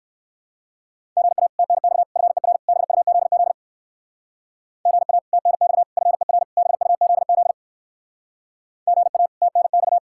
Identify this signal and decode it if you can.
Morse code, heard as a keyed tone, about 59 wpm, 700 Hz. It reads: GA TNX FER BUZZ GA TNX FER BUZZ GA TNX